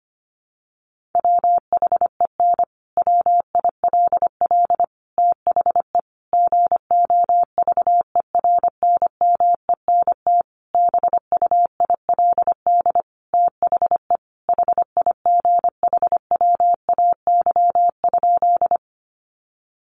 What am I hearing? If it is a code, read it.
Morse code, heard as a keyed tone, about 25 wpm, 700 Hz. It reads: WHEN WILL THE GOVERNMENT BUILD THE HIGHWAY?
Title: Sentences from Top 1400 Words 25wpm